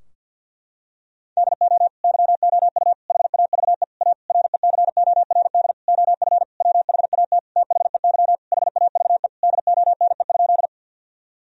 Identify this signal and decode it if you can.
Morse code, heard as a keyed tone, about 50 wpm, 700 Hz. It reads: DO YOU HAVE A RECORD OF WHAT THEY HAVE DONE?
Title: Sentences from Top 1000 Words - Review Part 3 - 50wpm